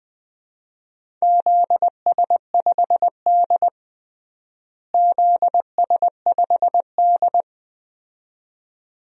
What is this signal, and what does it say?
Morse code, heard as a keyed tone, about 20 wpm, 700 Hz. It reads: ZS5D ZS5D
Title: DXCC-100 Call Signs - 4 Characters - Repeated 2x - 20wpm